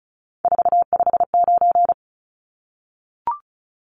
Morse code, heard as a keyed tone, about 34 wpm, 700 Hz. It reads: 459
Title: Numbers - 3-Digit 35wpm